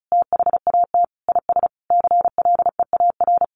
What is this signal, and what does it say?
Morse code, heard as a keyed tone, about 34 wpm, 700 Hz. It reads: THAT IS CLEAR
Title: Sentences from Top 900 Words 35wpm